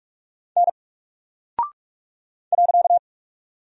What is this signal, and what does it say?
Morse code, heard as a keyed tone, about 44 wpm, 700 Hz. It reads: N .
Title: Single Character (Letter:Number:Punctuation:Prosign) Alt 45wpm